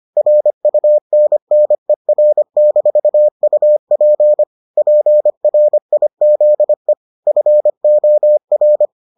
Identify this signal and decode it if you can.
Morse code, heard as a keyed tone, about 25 wpm, 600 Hz. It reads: RUNNER-UP PRIZE FOR